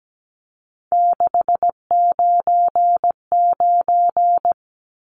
Morse code, heard as a keyed tone, about 17 wpm, 700 Hz. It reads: 699
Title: Numbers - 3-Digit 17wpm